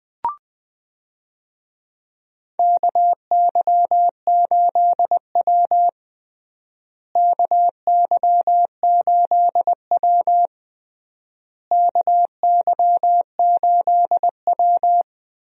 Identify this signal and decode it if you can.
Morse code, heard as a keyed tone, about 20 wpm, 700 Hz. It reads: KY8W KY8W KY8W
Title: USA Call Signs - 4 Characters - Repeated 3x - 20wpm